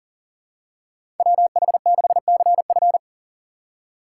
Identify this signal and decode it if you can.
Morse code, heard as a keyed tone, about 40 wpm, 700 Hz. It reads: WH6CF
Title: USA Call Signs - 5 Characters - 6x Character Spacing - 40wpm